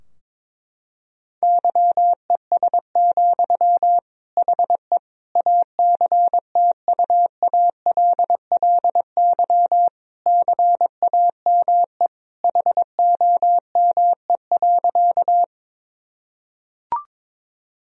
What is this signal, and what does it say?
Morse code, heard as a keyed tone, about 22 wpm, 700 Hz. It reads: YES, HE ACTUALLY CAME HOME.
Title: Sentences from Top 1000 Words - Review Part 4 - 22wpm